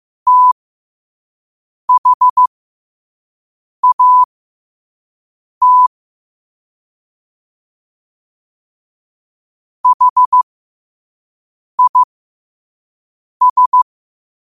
Morse code, heard as a keyed tone, about 15 wpm, spaced as for 5 wpm, 1000 Hz. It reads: THAT HIS
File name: qotdfiles / 5wpm